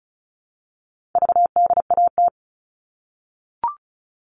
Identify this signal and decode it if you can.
Morse code, heard as a keyed tone, about 34 wpm, 700 Hz. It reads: VDAT
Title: All Cut Numbers - 4-Digit 35wpm